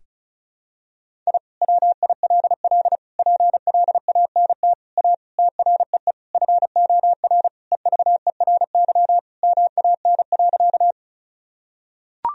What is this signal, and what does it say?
Morse code, heard as a keyed tone, about 34 wpm, 700 Hz. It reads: I WILL PLANT A TREE FOR EVERY MAN.
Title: Sentences from Top 1000 Words - Review Part 1 - 35wpm